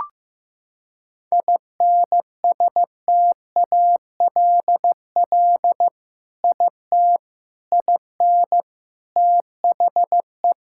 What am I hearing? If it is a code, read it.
Morse code, heard as a keyed tone, about 15 wpm, 700 Hz. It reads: INSTALL IT IN THE
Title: Sentences from Top 1600 Words 15wpm